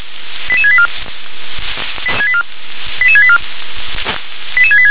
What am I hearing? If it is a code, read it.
Five-tone selective call (ZVEI), heard as quick runs of five tones, tones of about 70 ms. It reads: 88684 88684 88686
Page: /357/SDR/ZVEI/88737/